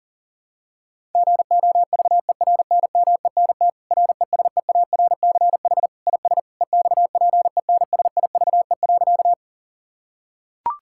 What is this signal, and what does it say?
Morse code, heard as a keyed tone, about 40 wpm, 700 Hz. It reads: GOVERNMENT RESEARCH IS EXPENSIVE.
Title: Sentences from Top 1000 Words - Review Part 6 - 40wpm